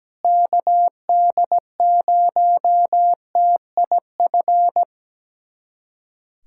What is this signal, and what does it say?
Morse code, heard as a keyed tone, about 17 wpm, 700 Hz. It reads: KD0TIF
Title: USA Callsign DE Callsign - 6 Characters - 17wpm